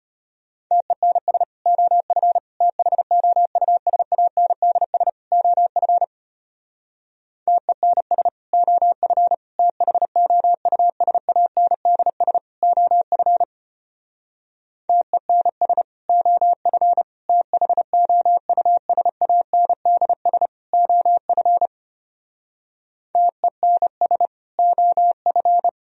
Morse code, heard as a keyed tone, about 33 wpm, 700 Hz. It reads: TENS OF THOUSANDS OF TENS OF THOUSANDS OF TENS OF THOUSANDS OF TENS OF